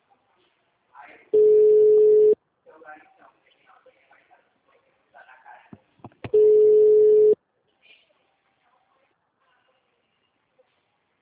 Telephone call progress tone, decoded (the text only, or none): ringback tone